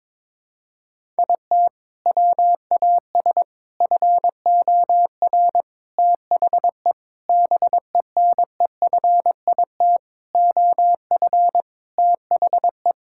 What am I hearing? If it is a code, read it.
Morse code, heard as a keyed tone, about 22 wpm, 700 Hz. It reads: IT WAS FOR THE BENEFIT OF THE
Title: Sentences from Top 1000 Words - Review Part 6 - 22wpm